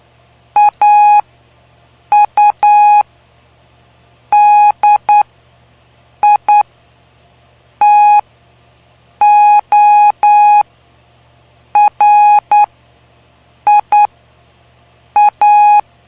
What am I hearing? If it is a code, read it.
Morse code, heard as a keyed tone, about 9 wpm, spaced as for 6 wpm, 830 Hz. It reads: AUDITORIA
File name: cw / py2lcd / 05-PY2LCD-Li��o 5